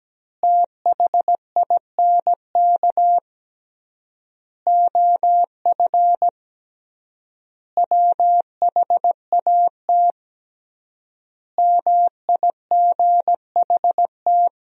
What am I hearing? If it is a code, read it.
Morse code, heard as a keyed tone, about 17 wpm, 700 Hz. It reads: THINK OF WHAT MIGHT
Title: 3x Word-Spacing - Sentences from Top 400 Words - 17wpm